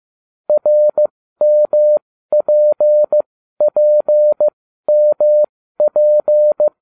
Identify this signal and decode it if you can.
Morse code, heard as a keyed tone, about 15 wpm, spaced as for 12 wpm, 600 Hz. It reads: RMPPMP